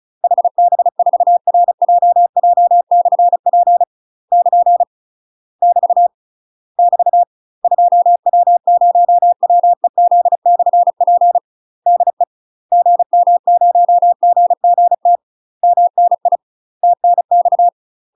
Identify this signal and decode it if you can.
Morse code, heard as a keyed tone, about 34 wpm, 700 Hz. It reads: HB4RJJ/P <KN> <BT> <BT> 2W0WEZ/P DE GM0GGT MNI TNX
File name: UK MASTER QSOs TRAINING/35WPM